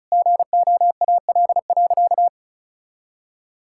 Morse code, heard as a keyed tone, about 36 wpm, 700 Hz. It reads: GOAL.
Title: Sentences from Top 1000 Words - Review Part 5 - 35wpm